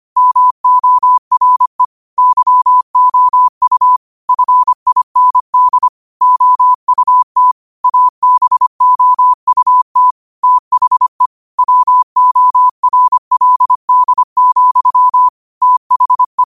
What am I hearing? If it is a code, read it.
Morse code, heard as a keyed tone, about 25 wpm, 1000 Hz. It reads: MORE YOU FIND OUT ABOUT THE WORLD, THE
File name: qotdfiles / 25wpm